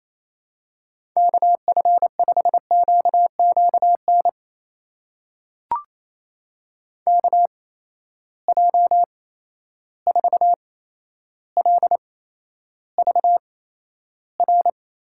Morse code, heard as a keyed tone, about 28 wpm, spaced as for 12 wpm, 700 Hz. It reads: KF5QQN KJ4LVR
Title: USA Call Signs - 6 Characters - 8x Character Spacing - 28wpm